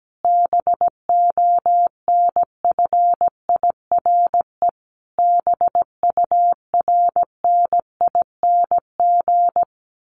Morse code, heard as a keyed tone, about 17 wpm, 700 Hz. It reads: BONFIRE BURNING
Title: A Sound Of Thunder 17wpm